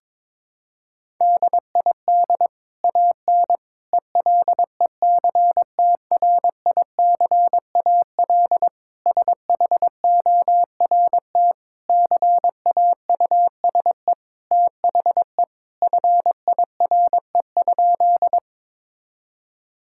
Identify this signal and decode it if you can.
Morse code, heard as a keyed tone, about 22 wpm, 700 Hz. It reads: DID AN ELECTRICAL SHORT CAUSE THE FIRE?
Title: Sentences from Top 1000 Words - Review Part 6 - 22wpm